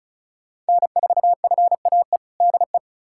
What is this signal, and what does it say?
Morse code, heard as a keyed tone, about 36 wpm, 700 Hz. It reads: N4FAE DE
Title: USA Callsign DE Callsign - 5 Characters - 35wpm